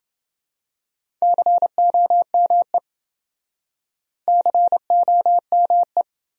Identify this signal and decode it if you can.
Morse code, heard as a keyed tone, about 27 wpm, 700 Hz. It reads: COME COME